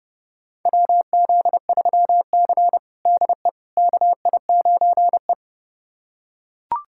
Morse code, heard as a keyed tone, about 32 wpm, 700 Hz. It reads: WZ3C DE KI9E
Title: USA Callsign DE Callsign - 4 Characters - 30wpm